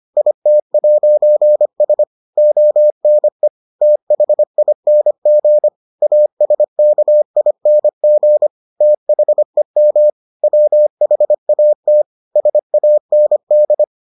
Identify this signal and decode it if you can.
Morse code, heard as a keyed tone, about 25 wpm, 600 Hz. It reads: IT'S ONE THING ASKING THEM WHAT SAND